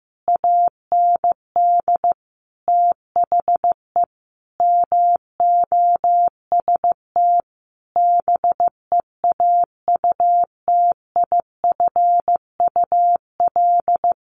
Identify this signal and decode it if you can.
Morse code, heard as a keyed tone, about 15 wpm, 700 Hz. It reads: AND THE MOST BEAUTIFUL
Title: Sentences from Top 1500 Words 15wpm